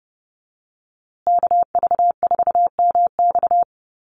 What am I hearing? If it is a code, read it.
Morse code, heard as a keyed tone, about 30 wpm, 700 Hz. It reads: KV4MX
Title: USA Call Signs Repeated 3x 30wpm